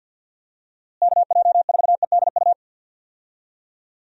Morse code, heard as a keyed tone, about 46 wpm, 700 Hz. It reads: KJ4EDU
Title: USA Call Signs - 6 Characters - Repeated 3x - 50wpm